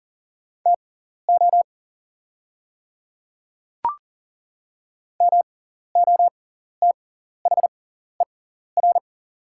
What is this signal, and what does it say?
Morse code, heard as a keyed tone, about 40 wpm, spaced as for 14 wpm, 700 Hz. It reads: TO MOTHER
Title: 6x Character Spacing - Top 200 Words - 40wpm